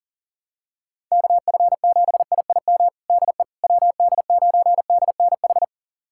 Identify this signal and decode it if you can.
Morse code, heard as a keyed tone, about 40 wpm, 700 Hz. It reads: KF7IIM DE WD9DNH
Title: USA Callsign DE Callsign - 6 Characters - 40wpm